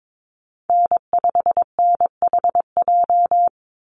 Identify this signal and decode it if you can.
Morse code, heard as a keyed tone, about 22 wpm, 700 Hz. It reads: N5NHJ
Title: Call Signs - 22wpm